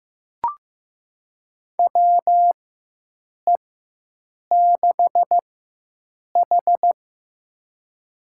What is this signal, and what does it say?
Morse code, heard as a keyed tone, about 15 wpm, spaced as for 7 wpm, 700 Hz. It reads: WE6H
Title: USA Call Signs - 4 Characters - 4x Character Spacing - 15wpm